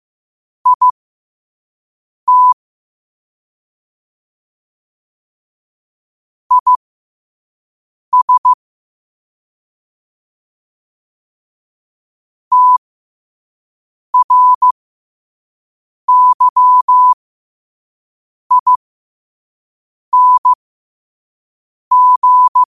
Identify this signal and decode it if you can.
Morse code, heard as a keyed tone, about 15 wpm, spaced as for 5 wpm, 1000 Hz. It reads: IT IS TRYING
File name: qotdfiles / 5wpm